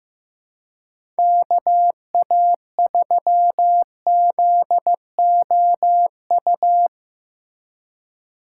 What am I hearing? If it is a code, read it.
Morse code, heard as a keyed tone, about 15 wpm, 700 Hz. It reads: KA3ZOU
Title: USA Call Signs - 6 Characters - 2x Character Spacing - 15wpm